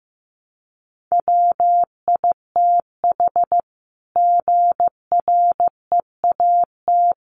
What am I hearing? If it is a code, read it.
Morse code, heard as a keyed tone, about 15 wpm, 700 Hz. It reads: WITH GREAT
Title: Sentences from Top 900 Words 15wpm